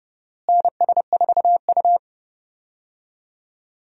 Morse code, heard as a keyed tone, about 32 wpm, 700 Hz. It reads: NS4U
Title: USA Call Signs - 4 Characters - Repeated 3x - 30wpm